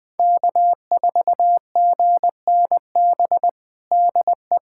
Morse code, heard as a keyed tone, about 21 wpm, 700 Hz. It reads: K4GNB DE
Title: USA Callsign DE Callsign - 5 Characters - 20wpm